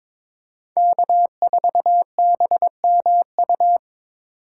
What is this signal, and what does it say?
Morse code, heard as a keyed tone, about 23 wpm, 700 Hz. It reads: K4BMU
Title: Speed Racing - USA 1x3 Call Signs 22wpm